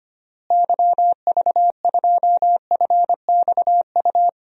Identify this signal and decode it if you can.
Morse code, heard as a keyed tone, about 25 wpm, 700 Hz. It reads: YV2FXU
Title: DXCC-100 Call Signs - 6 Characters - Repeated 2x - 25wpm